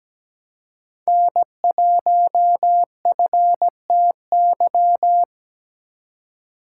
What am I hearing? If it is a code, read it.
Morse code, heard as a keyed tone, about 20 wpm, 700 Hz. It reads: N1FTY
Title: Speed Racing - USA 1x3 Call Signs 17wpm